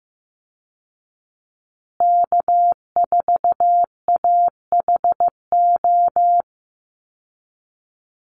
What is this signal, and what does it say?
Morse code, heard as a keyed tone, about 15 wpm, 700 Hz. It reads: K4AHO
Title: Call Signs Repeated 2x - 15wpm